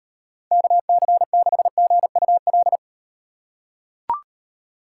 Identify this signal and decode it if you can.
Morse code, heard as a keyed tone, about 38 wpm, 700 Hz. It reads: KC6GUL